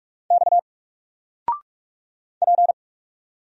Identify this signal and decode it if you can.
Morse code, heard as a keyed tone, about 44 wpm, 700 Hz. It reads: X P
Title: Single Character (Letter:Number:Punctuation:Prosign) Alt 45wpm